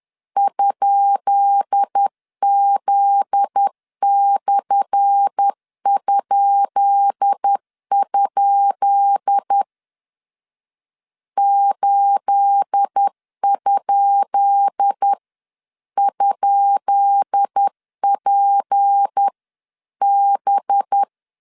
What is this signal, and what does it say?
Morse code, heard as a keyed tone, about 10 wpm, 800 Hz. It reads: ?Z/?? 8? ?PB
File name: LEZIONI / 10 SIGNS / 10B INT 10wpm 2844